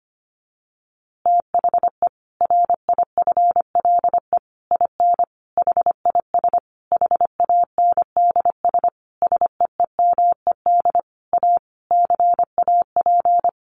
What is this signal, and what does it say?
Morse code, heard as a keyed tone, about 25 wpm, 700 Hz. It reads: THE RIFLE IN HIS HANDS SEEMED A CAP
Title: A Sound Of Thunder 25wpm